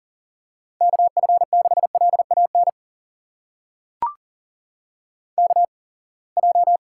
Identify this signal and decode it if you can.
Morse code, heard as a keyed tone, about 38 wpm, spaced as for 27 wpm, 700 Hz. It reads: KF6LAN KJ
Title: USA Call Signs - 6 Characters - 8x Character Spacing - 40wpm